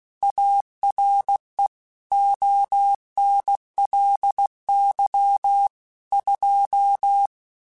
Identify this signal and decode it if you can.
Morse code, heard as a keyed tone, about 16 wpm, 800 Hz. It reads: ARE ONLY 2
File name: Technical Subjects / Technical Subjects-mp3s / AND Gate @15